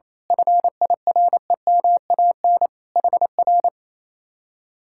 Morse code, heard as a keyed tone, about 27 wpm, 700 Hz. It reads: FIREMAN HR
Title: Speed-Racing Common QSO Phrases - 28wpm